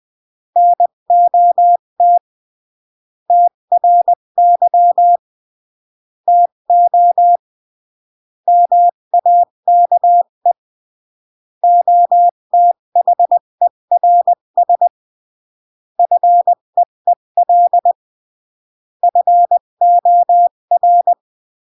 Morse code, heard as a keyed tone, about 20 wpm, 700 Hz. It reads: NOT TRY TO MAKE OTHERS FEEL FOR
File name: CW / jodorowsky / 82 commandments 20-15